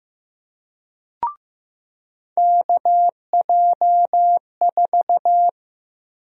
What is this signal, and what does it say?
Morse code, heard as a keyed tone, about 15 wpm, 700 Hz. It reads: KJ4
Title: USA Call Sign Prefixes - Repeated 2x - 15wpm